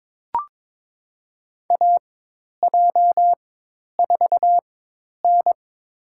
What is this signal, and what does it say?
Morse code, heard as a keyed tone, about 22 wpm, spaced as for 10 wpm, 700 Hz. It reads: AJ4N
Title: USA Call Signs - 4 Characters - 4x Character Spacing - 22wpm